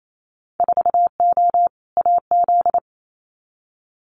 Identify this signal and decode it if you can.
Morse code, heard as a keyed tone, about 28 wpm, 700 Hz. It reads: 4O AZ
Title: Winter Field Day Exchanges Encore 28wpm